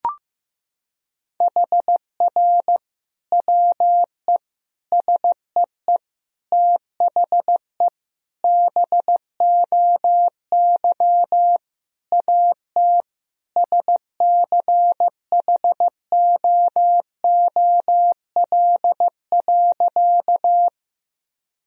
Morse code, heard as a keyed tone, about 15 wpm, 700 Hz. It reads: HR WE SEE THE BOY AT SCHOOL.
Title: Sentences from Top 300 Words and Abbreviations 15wpm